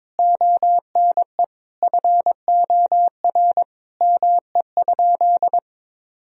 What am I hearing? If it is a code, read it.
Morse code, heard as a keyed tone, about 22 wpm, 700 Hz. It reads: ONE FOR ME?
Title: Sentences from Top 1000 Words - Review Part 3 - 22wpm